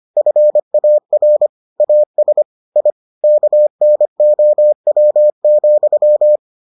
Morse code, heard as a keyed tone, about 25 wpm, 600 Hz. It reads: FAR AS I KNOW,